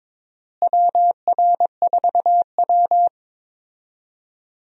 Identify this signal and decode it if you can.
Morse code, heard as a keyed tone, about 22 wpm, 700 Hz. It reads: WR4W
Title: USA Call Signs - 4 Characters - Repeated 3x - 22wpm